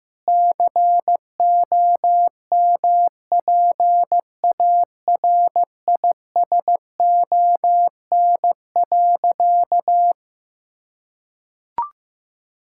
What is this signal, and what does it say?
Morse code, heard as a keyed tone, about 15 wpm, 700 Hz. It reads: COMPARISON.
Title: Sentences from Top 1600 Words 15wpm